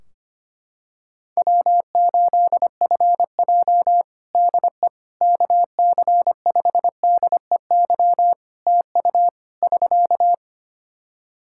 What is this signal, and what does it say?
Morse code, heard as a keyed tone, about 25 wpm, 700 Hz. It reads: W8FJ DE KC5DEY TU <SK>
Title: Practice QSO - Exchange 4 - Ending - 25wpm